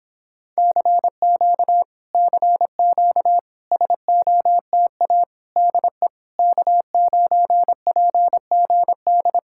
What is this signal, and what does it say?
Morse code, heard as a keyed tone, about 26 wpm, 700 Hz. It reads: CQ CQ SOTA DE K9PGD